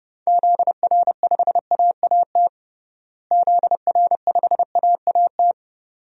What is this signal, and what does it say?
Morse code, heard as a keyed tone, about 32 wpm, 700 Hz. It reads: ZR5AAT ZR5AAT
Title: DXCC-100 Call Signs - 6 Characters - Repeated 2x - 30wpm